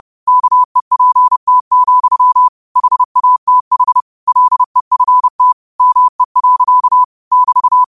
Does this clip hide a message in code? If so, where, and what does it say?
Yes: Morse code from 0.3 to 7.8 s, heard as a keyed tone, about 30 wpm, 1000 Hz. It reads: KEPT, HATH LEFT ME. <BT>